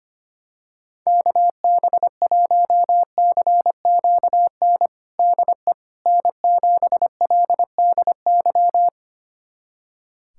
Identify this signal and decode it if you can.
Morse code, heard as a keyed tone, about 25 wpm, 700 Hz. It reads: KB1CQN DE N7LDY
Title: USA Callsign DE Callsign 25wpm